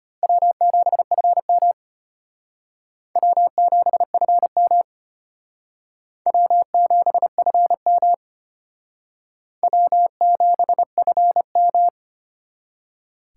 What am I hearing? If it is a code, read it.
Morse code, heard as a keyed tone, about 34 wpm, 700 Hz. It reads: W7FM W7FM W7FM W7FM